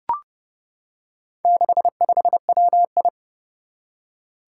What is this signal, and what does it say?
Morse code, heard as a keyed tone, about 31 wpm, 700 Hz. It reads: 65WI
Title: Four Challenging Letter-Number Combinations 30wpm